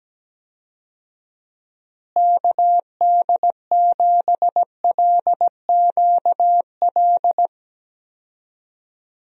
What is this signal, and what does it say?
Morse code, heard as a keyed tone, about 17 wpm, 700 Hz. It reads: KD7LQL
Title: USA Call Signs - 6 Characters - Repeated 2x - 17wpm